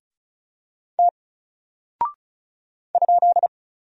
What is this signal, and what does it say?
Morse code, heard as a keyed tone, about 34 wpm, 700 Hz. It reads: T ?
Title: Single Character (Letter:Number:Punctuation:Prosign) Alt 35wpm